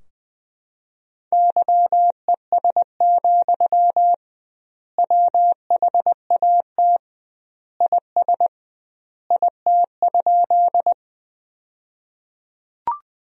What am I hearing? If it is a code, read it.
Morse code, heard as a keyed tone, about 20 wpm, 700 Hz. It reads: YES, WHAT IS IT?
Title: 2x Word-Spacing - Sentences from Top 500 Words - 20wpm